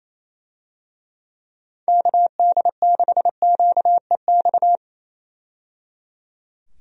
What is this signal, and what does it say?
Morse code, heard as a keyed tone, about 28 wpm, 700 Hz. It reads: KD6QEX
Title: USA Call Signs - 6 Characters - 28wpm